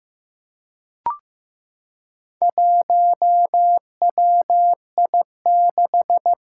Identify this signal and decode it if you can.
Morse code, heard as a keyed tone, about 15 wpm, 700 Hz. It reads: 1WI6
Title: Four Challenging Letter-Number Combinations 15wpm